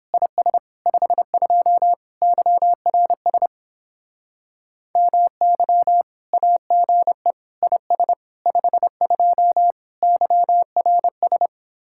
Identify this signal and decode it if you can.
Morse code, heard as a keyed tone, about 31 wpm, 700 Hz. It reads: IS 52 YRS MY AGE IS 52 YRS